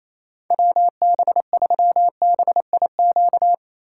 Morse code, heard as a keyed tone, about 28 wpm, 700 Hz. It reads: WB3BIQ
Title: K1USN SST QSO Practice - 28wpm